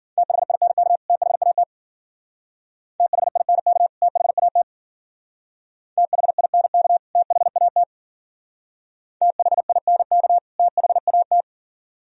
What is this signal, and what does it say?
Morse code, heard as a keyed tone, about 51 wpm, 700 Hz. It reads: THINK THAT THINK THAT THINK THAT THINK THAT